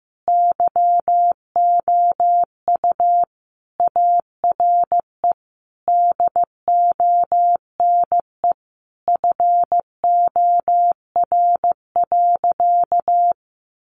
Morse code, heard as a keyed tone, about 15 wpm, 700 Hz. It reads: YOU ARE DONE FOR.
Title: Sentences from Top 1500 Words 15wpm